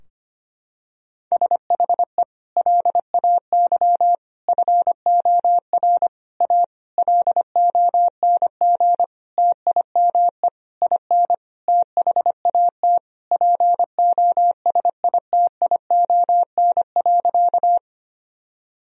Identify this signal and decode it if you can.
Morse code, heard as a keyed tone, about 25 wpm, 700 Hz. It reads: SHE LAY FOR A LONG TIME IN THAT POSITION.
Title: Sentences from Top 1000 Words - Review Part 2 - 25wpm